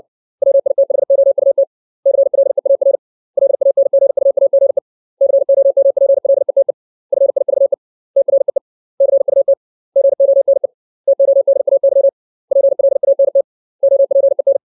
Morse code, heard as a keyed tone, about 60 wpm, 550 Hz. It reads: WITHOUT CLEAR LANGUAGE COMPLETE FIVE TREE CUT GONE TODAY PLANT OPEN
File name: 632 words / 632 Words 60WPM